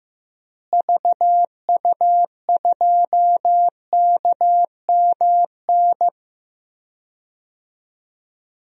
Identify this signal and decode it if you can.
Morse code, heard as a keyed tone, about 15 wpm, 700 Hz. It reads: VU2KMN
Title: DXCC-100 Call Signs - 6 Characters - Repeated 2x - 15wpm